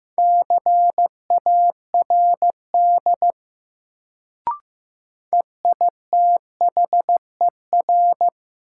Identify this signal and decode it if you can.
Morse code, heard as a keyed tone, about 15 wpm, 700 Hz. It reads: CARD EITHER
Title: Sentences from Top 600 Words - New Word Drill - 15wpm